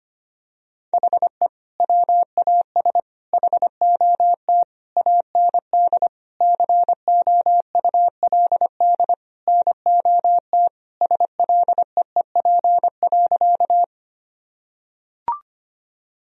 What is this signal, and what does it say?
Morse code, heard as a keyed tone, about 25 wpm, 700 Hz. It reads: HE WAS HOT AND COULD NOT SLEEP.
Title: Sentences from Top 1000 Words - Review Part 3 - 25wpm